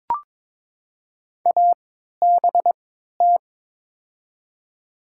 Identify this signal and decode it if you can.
Morse code, heard as a keyed tone, about 22 wpm, spaced as for 12 wpm, 700 Hz. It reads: ABT
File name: Top 90 QSO Elements - 3x Character Spacing - 22wpm